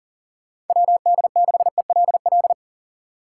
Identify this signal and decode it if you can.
Morse code, heard as a keyed tone, about 50 wpm, 700 Hz. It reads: WD6ELL
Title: Speed Racing - USA 2x3 Call Signs 40wpm